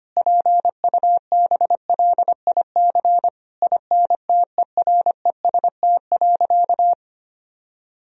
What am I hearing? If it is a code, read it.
Morse code, heard as a keyed tone, about 25 wpm, 700 Hz. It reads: PUBLIC INTEREST.
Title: Sentences from Top 1000 Words - Review Part 4 - 25wpm